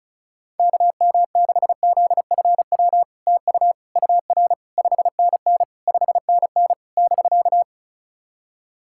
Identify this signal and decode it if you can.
Morse code, heard as a keyed tone, about 36 wpm, 700 Hz. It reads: KM6ZFW TU UR 5NN 5NN <BK>
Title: SOTA QSO Practice 35wpm